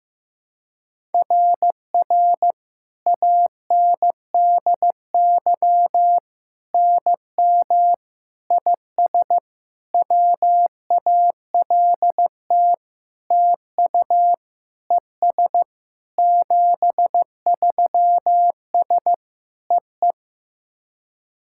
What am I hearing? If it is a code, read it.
Morse code, heard as a keyed tone, about 15 wpm, 700 Hz. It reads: RR ANDY NM IS WALT TU ES 73S EE
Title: SOTA QSO Practice 15wpm